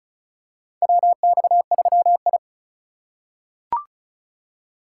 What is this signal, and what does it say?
Morse code, heard as a keyed tone, about 34 wpm, 700 Hz. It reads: WX3I
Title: USA Call Signs - 4 Characters - 3x Character Spacing - 35wpm